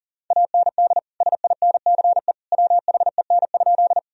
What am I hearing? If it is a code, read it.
Morse code, heard as a keyed tone, about 40 wpm, 700 Hz. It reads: AND SINCE WHEN?
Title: Sentences from Top 1000 Words - Review Part 1 - 40wpm